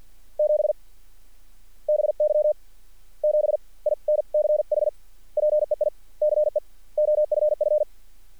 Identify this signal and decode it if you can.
Morse code, heard as a keyed tone, about 50 wpm, spaced as for 40 wpm, 600 Hz. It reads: 7 BY 7 INCH PEICE OFF
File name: archived / CRSnet